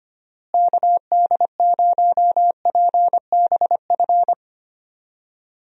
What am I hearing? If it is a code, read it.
Morse code, heard as a keyed tone, about 25 wpm, 700 Hz. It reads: KD0PBF
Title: USA Call Signs - 6 Characters - Repeated 3x - 25wpm